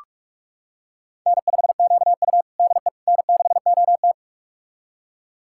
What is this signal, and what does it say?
Morse code, heard as a keyed tone, about 46 wpm, 700 Hz. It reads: N5QU DE N6OT
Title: USA Callsign DE Callsign - 4 Characters - 45wpm